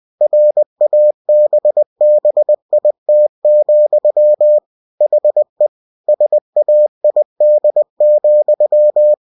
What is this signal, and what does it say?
Morse code, heard as a keyed tone, about 20 wpm, 600 Hz. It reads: RABBIT, HE SAID,